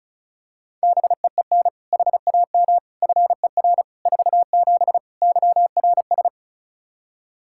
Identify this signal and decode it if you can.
Morse code, heard as a keyed tone, about 36 wpm, 700 Hz. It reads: BEEN HAM FER 47 YRS
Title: Practice QSO - Exchange 3 - More Info - 35wpm